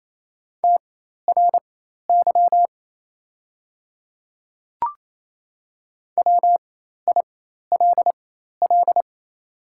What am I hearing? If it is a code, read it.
Morse code, heard as a keyed tone, about 28 wpm, spaced as for 13 wpm, 700 Hz. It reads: TRY WILL
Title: 4x Character Spacing - Top 300 Words - 28wpm